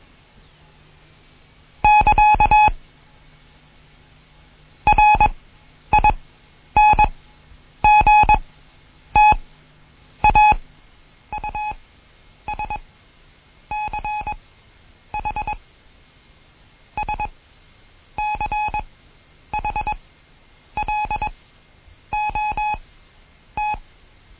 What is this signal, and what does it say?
Morse code, heard as a keyed tone, about 21 wpm, spaced as for 9 wpm, 840 Hz. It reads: <KA> RINGTAUSCH SCHLOT
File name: Morsen / Kurs A 20